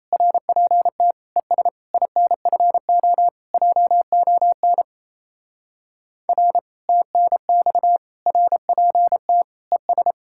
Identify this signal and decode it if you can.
Morse code, heard as a keyed tone, about 34 wpm, 700 Hz. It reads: RPT ES INFO JON R TNX RPT ES
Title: Speed-Racing Common QSO Phrases - 28wpm